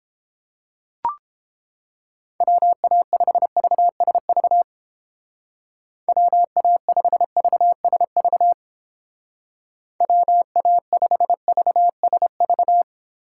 Morse code, heard as a keyed tone, about 29 wpm, 700 Hz. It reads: WA5VSV WA5VSV WA5VSV